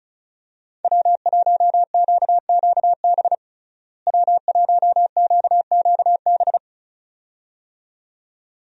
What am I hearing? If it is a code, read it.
Morse code, heard as a keyed tone, about 36 wpm, 700 Hz. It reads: W1QQB W1QQB
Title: USA Call Signs - 5 Characters - Repeated 2x - 35wpm